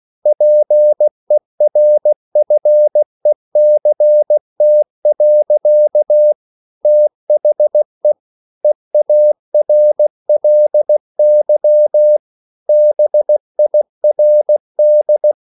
Morse code, heard as a keyed tone, about 16 wpm, 600 Hz. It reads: PERFECT. THE EARLY BIRD
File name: CW / Practice / EnglishSayings16